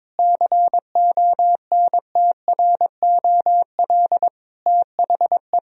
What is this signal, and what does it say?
Morse code, heard as a keyed tone, about 22 wpm, 700 Hz. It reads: CONTROL THE
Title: Sentences from Top 1000 Words - Review Part 3 - 22wpm